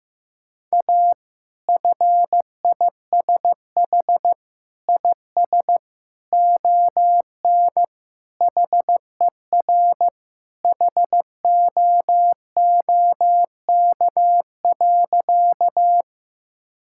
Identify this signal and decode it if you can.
Morse code, heard as a keyed tone, about 15 wpm, 700 Hz. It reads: A FISH IS ON HER HOOK.
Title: Sentences from Top 1600 Words 15wpm